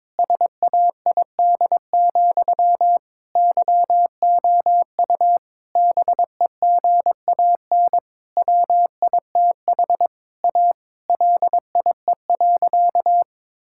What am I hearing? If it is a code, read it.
Morse code, heard as a keyed tone, about 22 wpm, 700 Hz. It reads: SAID, YOU BEGAN WITH A LIE.
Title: Sentences from Top 1600 Words 22wpm